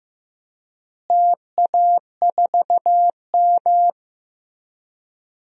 Morse code, heard as a keyed tone, about 15 wpm, 700 Hz. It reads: TA4M
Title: DXCC-100 Call Signs - 4 Characters - Repeated 3x - 15wpm